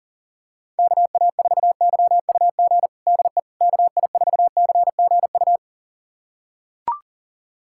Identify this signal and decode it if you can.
Morse code, heard as a keyed tone, about 40 wpm, 700 Hz. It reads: KA4YUG DE KI4CGU
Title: USA Callsign DE Callsign - 6 Characters - 40wpm